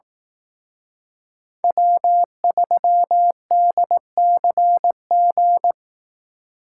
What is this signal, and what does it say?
Morse code, heard as a keyed tone, about 18 wpm, 700 Hz. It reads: W3DCG
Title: Speed Racing - USA 1x3 Call Signs 15wpm